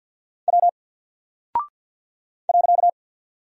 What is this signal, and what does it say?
Morse code, heard as a keyed tone, about 48 wpm, 700 Hz. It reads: W .
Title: Single Character (Letter:Number:Punctuation:Prosign) Alt 50wpm